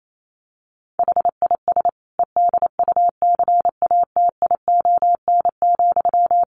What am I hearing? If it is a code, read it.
Morse code, heard as a keyed tone, about 28 wpm, 700 Hz. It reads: HIS EDUCATION,
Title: Sentences from Top 1400 Words 28wpm